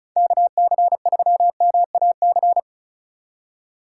Morse code, heard as a keyed tone, about 36 wpm, 700 Hz. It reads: KC3MAC
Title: USA Call Signs - 6 Characters - 2x Character Spacing - 35wpm